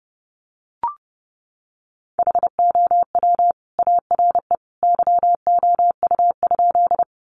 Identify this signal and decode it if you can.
Morse code, heard as a keyed tone, about 30 wpm, 700 Hz. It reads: HOW ARE YOU?
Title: Sentences from Top 100 Words 30wpm